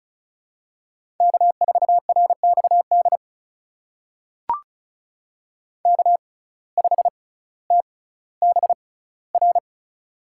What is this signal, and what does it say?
Morse code, heard as a keyed tone, about 36 wpm, spaced as for 18 wpm, 700 Hz. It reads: K4RXD K5TBR
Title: USA Call Signs - 5 Characters - 6x Character Spacing - 35wpm